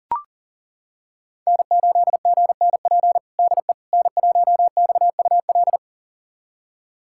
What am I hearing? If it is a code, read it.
Morse code, heard as a keyed tone, about 37 wpm, 700 Hz. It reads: N8GNP DE N1XUL
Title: USA Callsign DE Callsign - 5 Characters - 40wpm